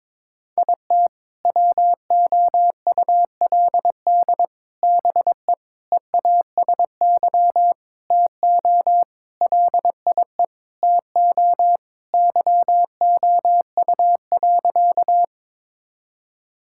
Morse code, heard as a keyed tone, about 22 wpm, 700 Hz. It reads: IT WOULD BE EASY TO LIE TO YOU.
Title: Sentences from Top 1600 Words 22wpm